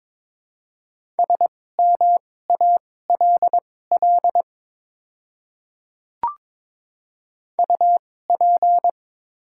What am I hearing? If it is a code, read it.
Morse code, heard as a keyed tone, about 23 wpm, spaced as for 16 wpm, 700 Hz. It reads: SMALL UP
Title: 2x Character Spacing - Top 200 Words - 22wpm